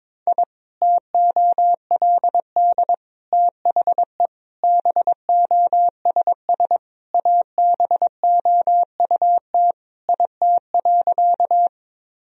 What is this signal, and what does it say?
Morse code, heard as a keyed tone, about 21 wpm, 700 Hz. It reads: I TOLD THE BOSS ABOUT IT.
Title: Sentences from Top 1000 Words - Review Part 4 - 22wpm